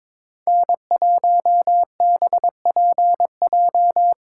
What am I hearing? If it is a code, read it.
Morse code, heard as a keyed tone, about 22 wpm, 700 Hz. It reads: N1BPJ
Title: USA Call Signs - 4x Character Spacing - 22wpm